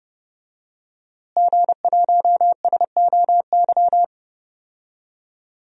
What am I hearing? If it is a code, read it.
Morse code, heard as a keyed tone, about 30 wpm, 700 Hz. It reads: G1SOY
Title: DXCC-100 Call Signs - 5 Characters - Repeated 2x - 30wpm